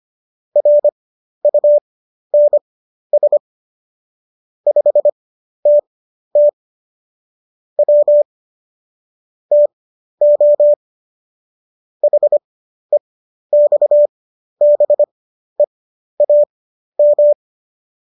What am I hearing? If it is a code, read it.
Morse code, heard as a keyed tone, about 25 wpm, spaced as for 12 wpm, 590 Hz. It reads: RUNS 5TT W TO HEXBEAM
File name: qso009 12